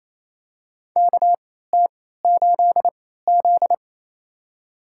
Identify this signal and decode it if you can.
Morse code, heard as a keyed tone, about 28 wpm, spaced as for 16 wpm, 700 Hz. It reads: KT8Z
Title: USA Call Signs - 3x Character Spacing - 28wpm